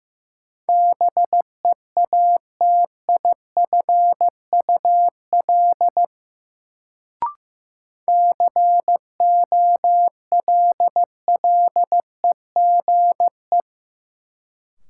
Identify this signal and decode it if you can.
Morse code, heard as a keyed tone, about 16 wpm, 700 Hz. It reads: BEATIFUL COLLEGE
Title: Sentences from Top 1300 Words - New Word Drill Alt - 15wpm